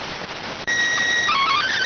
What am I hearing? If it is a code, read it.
SSTV (slow-scan television), PD 290.